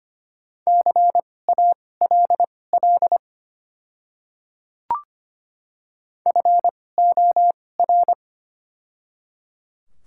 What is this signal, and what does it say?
Morse code, heard as a keyed tone, about 25 wpm, spaced as for 18 wpm, 700 Hz. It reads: CALL FOR
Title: 2x Character Spacing - Top 300 Words - 25wpm